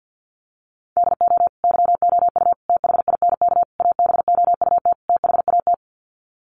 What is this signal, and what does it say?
Morse code, heard as a keyed tone, about 50 wpm, 700 Hz. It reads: DO YOU THINK ABOUT THAT
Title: Sets of 5 Words Encore2 50wpm